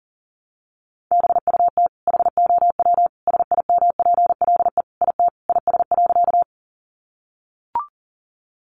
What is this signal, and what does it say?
Morse code, heard as a keyed tone, about 40 wpm, 700 Hz. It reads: BUT HOW SIMPLE IT IS.
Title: Sentences from Top 400 Words 40wpm